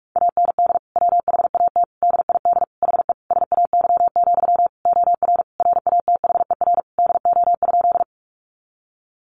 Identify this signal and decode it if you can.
Morse code, heard as a keyed tone, about 44 wpm, 700 Hz. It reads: AND WHAT DID HE SAY, OR RATHER DO?
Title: Sentences from Top 700 Words 45wpm